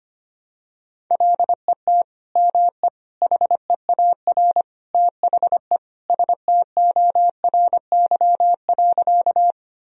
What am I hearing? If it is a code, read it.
Morse code, heard as a keyed tone, about 25 wpm, 700 Hz. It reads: LET ME HEAR THE STORY.
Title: Sentences from Top 1000 Words - Review Part 1 - 25wpm